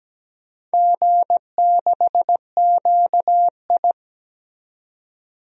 Morse code, heard as a keyed tone, about 17 wpm, 700 Hz. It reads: G6QI
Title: DXCC-100 Call Signs - 4 Characters - 17wpm